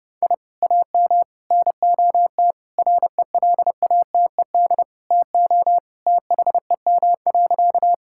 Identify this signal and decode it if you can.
Morse code, heard as a keyed tone, about 30 wpm, 700 Hz. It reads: I AM NOT RELATED TO THEM.
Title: Sentences from Top 1000 Words - Review Part 5 - 30wpm